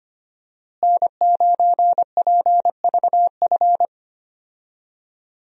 Morse code, heard as a keyed tone, about 25 wpm, 700 Hz. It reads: N9PVF
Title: USA Call Signs - 5 Characters - Repeated 2x - 25wpm